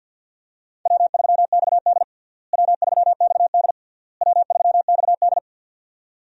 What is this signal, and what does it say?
Morse code, heard as a keyed tone, about 47 wpm, 700 Hz. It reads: W3XD W3XD W3XD
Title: USA Call Signs - 4 Characters - Repeated 3x - 50wpm